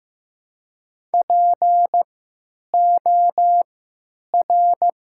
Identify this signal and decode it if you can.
Morse code, heard as a keyed tone, about 15 wpm, 700 Hz. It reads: P O R
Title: 3x Character Spacing - Top 200 Words - 15wpm